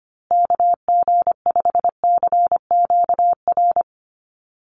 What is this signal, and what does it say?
Morse code, heard as a keyed tone, about 25 wpm, 700 Hz. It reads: KG5CQR
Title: USA Call Signs Repeated 3x 25wpm